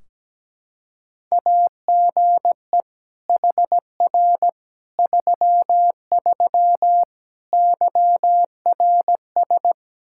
Morse code, heard as a keyed tone, about 17 wpm, 700 Hz. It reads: AGE HR 33 YRS
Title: Practice QSO - Exchange 3 - More Info - 17wpm